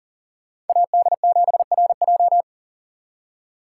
Morse code, heard as a keyed tone, about 52 wpm, 700 Hz. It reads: AD7RJ